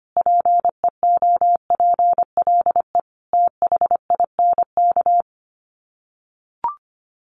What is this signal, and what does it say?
Morse code, heard as a keyed tone, about 25 wpm, 700 Hz. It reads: PEOPLE THINK
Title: Sets of 5 Words Encore 25wpm